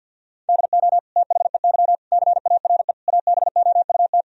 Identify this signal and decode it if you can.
Morse code, heard as a keyed tone, about 52 wpm, 700 Hz. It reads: DO THEY CARE ABOUT
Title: Sentences from Top 1000 Words - Review Part 3 - 50wpm